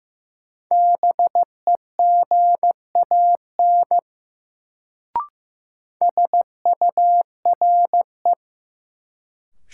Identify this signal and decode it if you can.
Morse code, heard as a keyed tone, about 15 wpm, 700 Hz. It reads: BEGAN SURE
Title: Sentences from Top 300 Words - New Word Drill - 15wpm